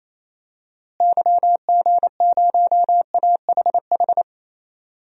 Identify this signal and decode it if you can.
Morse code, heard as a keyed tone, about 28 wpm, 700 Hz. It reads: YG0AHH
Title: DXCC-100 Call Signs - 6 Characters - Repeated 3x - 28wpm